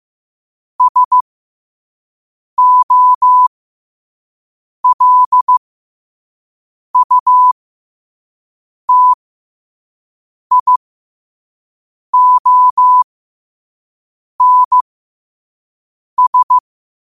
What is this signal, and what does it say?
Morse code, heard as a keyed tone, about 15 wpm, spaced as for 5 wpm, 1000 Hz. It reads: SOLUTIONS
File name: qotdfiles / 5wpm